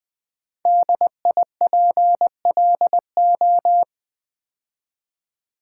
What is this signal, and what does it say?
Morse code, heard as a keyed tone, about 20 wpm, 700 Hz. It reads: DIPLO
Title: Build A Word - 5 Letters At A Time - 10-Letter Words - 20wpm